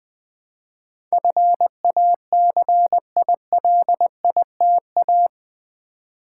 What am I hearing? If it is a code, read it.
Morse code, heard as a keyed tone, about 21 wpm, 700 Hz. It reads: FACILITA
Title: Build A Word - 4 Letters At A Time - 10-Letter Words - 20wpm